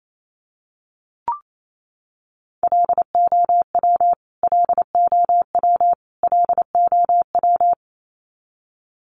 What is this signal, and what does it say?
Morse code, heard as a keyed tone, about 28 wpm, 700 Hz. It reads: LOW LOW LOW
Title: Top 300 Words Repeated 28wpm